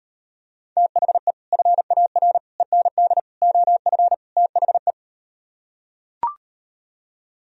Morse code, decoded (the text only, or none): THE FAR END OF THE